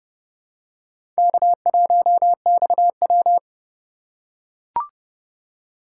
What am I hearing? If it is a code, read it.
Morse code, heard as a keyed tone, about 29 wpm, 700 Hz. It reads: K1XW